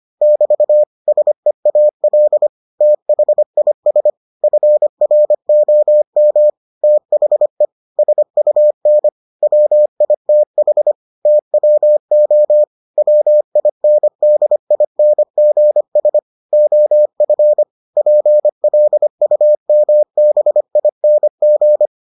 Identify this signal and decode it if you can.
Morse code, heard as a keyed tone, about 25 wpm, 600 Hz. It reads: <BT> SEAL THIS FROM THE SUN WITH TWO WINDINGS OF PLUMBING